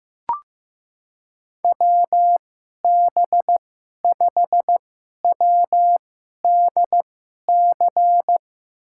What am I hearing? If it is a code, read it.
Morse code, heard as a keyed tone, about 15 wpm, spaced as for 11 wpm, 700 Hz. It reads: WB5WDC
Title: USA Call Signs - 6 Characters - 2x Character Spacing - 15wpm